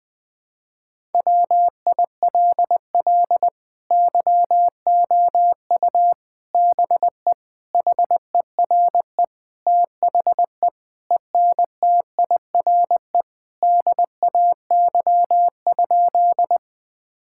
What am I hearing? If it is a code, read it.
Morse code, heard as a keyed tone, about 20 wpm, 700 Hz. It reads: WILL YOU BE HERE THE ENTIRE DAY?
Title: Sentences from Top 1000 Words - Review Part 5 - 20wpm